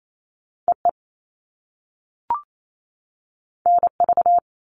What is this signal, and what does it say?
Morse code, heard as a keyed tone, about 28 wpm, 700 Hz. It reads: EE NV
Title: All Cut Numbers - 2-Digit 28wpm